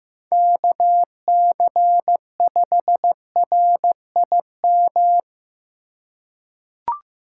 Morse code, heard as a keyed tone, about 15 wpm, 700 Hz. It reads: KC5RIM
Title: USA Call Signs - 4x Character Spacing - 15wpm